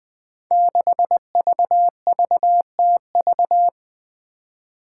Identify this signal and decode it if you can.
Morse code, heard as a keyed tone, about 20 wpm, 700 Hz. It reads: 6VVTV
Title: Five Challenging Letter-Number Combinations 20wpm